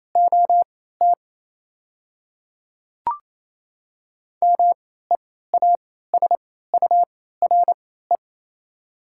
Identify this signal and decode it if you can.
Morse code, heard as a keyed tone, about 28 wpm, spaced as for 16 wpm, 700 Hz. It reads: OT MEASURE
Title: 3x Character Spacing - Top 500 Words - 28wpm